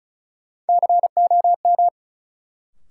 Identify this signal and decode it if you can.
Morse code, heard as a keyed tone, about 36 wpm, 700 Hz. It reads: COM
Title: Sentences from Top 1300 Words - New Word Drill - 35wpm